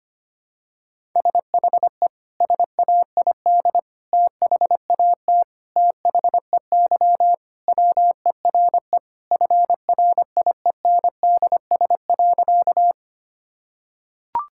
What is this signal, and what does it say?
Morse code, heard as a keyed tone, about 25 wpm, 700 Hz. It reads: SHE SAID THAT THEY WERE FRIENDS.
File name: practice / 25wpm / Sentences with 3-Word Phrases - 25wpm